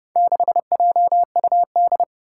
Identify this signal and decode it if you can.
Morse code, heard as a keyed tone, about 29 wpm, 700 Hz. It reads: 6JUD
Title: Four Challenging Letter-Number Combinations 30wpm